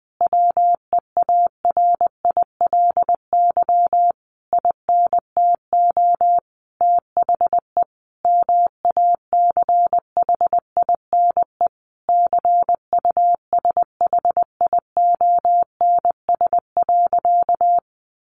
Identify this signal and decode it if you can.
Morse code, heard as a keyed tone, about 20 wpm, 700 Hz. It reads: WEARILY INTO THE MACHINE CUSHIONS.
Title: A Sound Of Thunder 20wpm